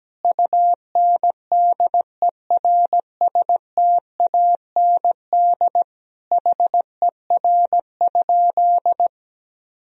Morse code, heard as a keyed tone, about 17 wpm, 700 Hz. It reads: UNDERSTAND HER?
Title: Sentences from Top 1000 Words - Review Part 2 - 17wpm